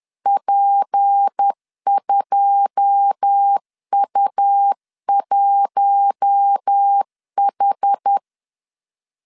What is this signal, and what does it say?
Morse code, heard as a keyed tone, about 10 wpm, 800 Hz. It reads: P2U1H
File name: LEZIONI / 9 28 10wpm 2842